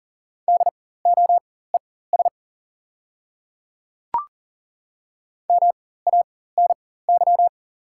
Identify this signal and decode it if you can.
Morse code, heard as a keyed tone, about 40 wpm, spaced as for 19 wpm, 690 Hz. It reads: DOES MANY
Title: 4x Character Spacing - Top 200 Words - 40wpm